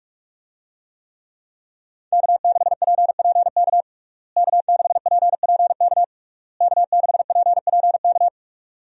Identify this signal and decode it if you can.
Morse code, heard as a keyed tone, about 46 wpm, 700 Hz. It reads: K6PPK K6PPK K6PPK
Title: USA Call Signs - 5 Characters - Repeated 3x - 45wpm